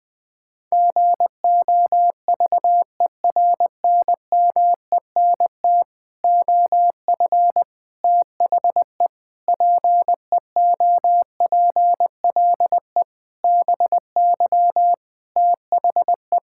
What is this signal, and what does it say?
Morse code, heard as a keyed tone, about 20 wpm, 700 Hz. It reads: GOVERNMENT OF THE PEOPLE BY THE
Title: Sets of 9 Words - 20wpm